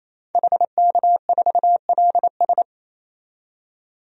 Morse code, heard as a keyed tone, about 28 wpm, 700 Hz. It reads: HK4LS
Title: DXCC-100 Call Signs - 5 Characters - 28wpm